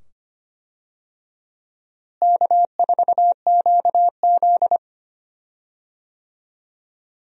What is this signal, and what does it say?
Morse code, heard as a keyed tone, about 25 wpm, 700 Hz. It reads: K4QZ
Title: USA Call Signs - 4 Characters - Repeated 2x - 25wpm